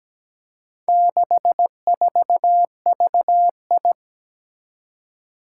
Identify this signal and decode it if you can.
Morse code, heard as a keyed tone, about 17 wpm, 700 Hz. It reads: 64VI
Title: Four Challenging Letter-Number Combinations 17wpm